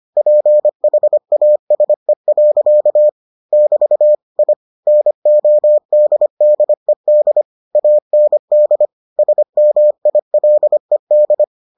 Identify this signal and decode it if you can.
Morse code, heard as a keyed tone, about 25 wpm, 600 Hz. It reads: PHASE. <BT> I NODDED AND SMILED